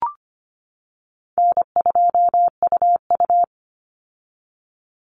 Morse code, heard as a keyed tone, about 25 wpm, 700 Hz. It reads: N2UU
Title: Easy Call Signs - 25wpm